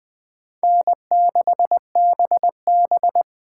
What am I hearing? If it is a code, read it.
Morse code, heard as a keyed tone, about 21 wpm, 700 Hz. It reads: N6BB
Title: Speed Racing - USA 1x2 Call Signs 17wpm